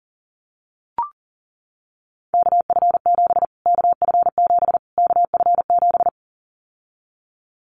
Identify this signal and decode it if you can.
Morse code, heard as a keyed tone, about 40 wpm, 700 Hz. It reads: KF7 KF7 KF7
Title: Callsign Prefixes for Top 25 DXCC - Repeated 3x - 40wpm